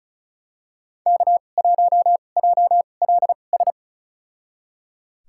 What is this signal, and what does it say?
Morse code, heard as a keyed tone, about 35 wpm, spaced as for 25 wpm, 690 Hz. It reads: K1JLS
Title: USA Call Signs - 5 Characters - 2x Character Spacing - 35wpm